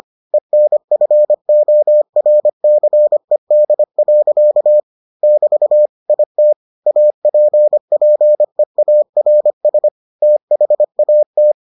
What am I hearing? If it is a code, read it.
Morse code, heard as a keyed tone, about 25 wpm, 600 Hz. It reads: ENFORCED. <BT> IT APPEARS THAT